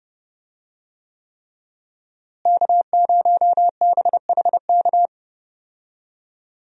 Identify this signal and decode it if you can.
Morse code, heard as a keyed tone, about 32 wpm, 700 Hz. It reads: K0BHK
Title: USA Call Signs - 5 Characters - Repeated 3x - 30wpm